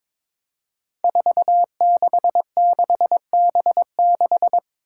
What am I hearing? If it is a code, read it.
Morse code, heard as a keyed tone, about 22 wpm, 700 Hz. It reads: 466B6
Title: Five Challenging Letter-Number Combinations 22wpm